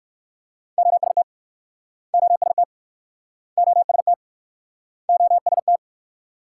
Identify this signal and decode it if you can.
Morse code, heard as a keyed tone, about 54 wpm, 690 Hz. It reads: OST OST OST OST
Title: Speed-Racing Multi-letter Phonemes - 45wpm